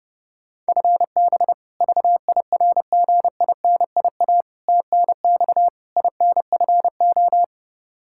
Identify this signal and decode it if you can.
Morse code, heard as a keyed tone, about 30 wpm, 700 Hz. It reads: FB VIRGINIA TNX INFO
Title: K1USN SST QSO Practice 30wpm